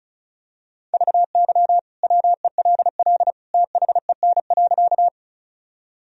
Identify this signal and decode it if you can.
Morse code, heard as a keyed tone, about 36 wpm, 700 Hz. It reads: VY WELL THEN.
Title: Sentences from Top 200 Words and Abbreviations 35wpm